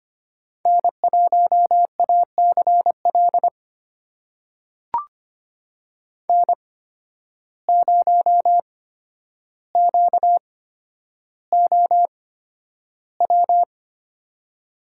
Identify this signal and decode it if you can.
Morse code, heard as a keyed tone, about 25 wpm, spaced as for 11 wpm, 700 Hz. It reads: N1ACL N0QOW
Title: USA Call Signs - 5 Characters - 8x Character Spacing - 25wpm